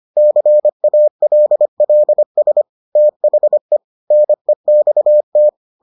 Morse code, heard as a keyed tone, about 25 wpm, 600 Hz. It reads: CALLS THE NEXT